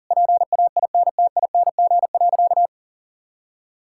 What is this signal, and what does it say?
Morse code, heard as a keyed tone, about 40 wpm, 700 Hz. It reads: PAINTING.
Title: Sentences from Top 1000 Words - Review Part 6 - 40wpm